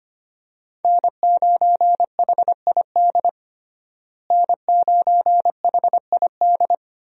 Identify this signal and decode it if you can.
Morse code, heard as a keyed tone, about 25 wpm, 700 Hz. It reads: N9HID N9HID
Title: USA Call Signs - 5 Characters - Repeated 2x - 25wpm